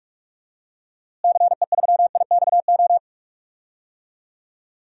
Morse code, heard as a keyed tone, about 46 wpm, 700 Hz. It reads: CE3IXO
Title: DXCC-100 Call Signs - 6 Characters - Repeated 2x - 45wpm